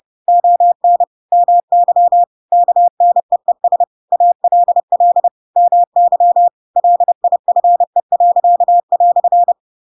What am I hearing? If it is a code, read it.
Morse code, heard as a keyed tone, about 30 wpm, 700 Hz. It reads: ON MY KNEES ALL MY LIFE."
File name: Pinocchio / Pinocchio ch7 30 30 ik1hge